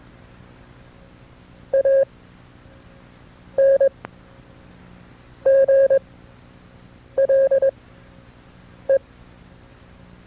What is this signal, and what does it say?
Morse code, heard as a keyed tone, about 21 wpm, spaced as for 6 wpm, 540 Hz. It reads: ANGLE
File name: FreeFiles / CW-practice / KB6MT-tape2-side2